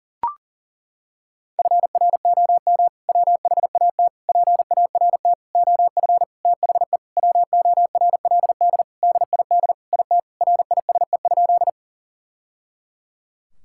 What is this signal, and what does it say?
Morse code, heard as a keyed tone, about 37 wpm, 700 Hz. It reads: FROM WHAT PART OF THE WORLD DID IT RISE?
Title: Sentences from Top 1000 Words - Review Part 6 - 40wpm